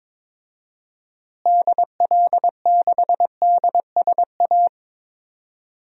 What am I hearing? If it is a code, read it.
Morse code, heard as a keyed tone, about 22 wpm, 700 Hz. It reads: DL6DSA
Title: DXCC-100 Call Signs - 6 Characters - Repeated 2x - 22wpm